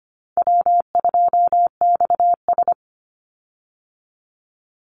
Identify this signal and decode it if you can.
Morse code, heard as a keyed tone, about 25 wpm, 700 Hz. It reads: W2XS
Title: Easy Call Signs Repeated 2x - 25wpm